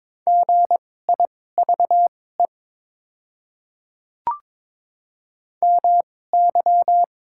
Morse code, heard as a keyed tone, about 22 wpm, spaced as for 16 wpm, 700 Hz. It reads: GIVE MY
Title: 2x Character Spacing - Top 200 Words - 22wpm